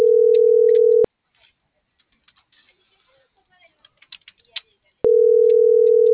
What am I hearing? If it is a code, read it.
Ringback tone (North America), 440 + 480 Hz.
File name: CENTRAL MORENA / out dulce.morena.2024